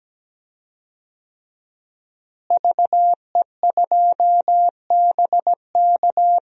Morse code, heard as a keyed tone, about 17 wpm, 700 Hz. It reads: VE2BK
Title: DXCC-100 Call Signs - 5 Characters - Repeated 3x - 17wpm